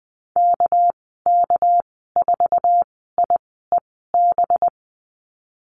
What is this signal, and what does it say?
Morse code, heard as a keyed tone, about 20 wpm, spaced as for 14 wpm, 700 Hz. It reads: KK4IEB
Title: 231 Lesson 37g - Letter x - Call Signs 20wpm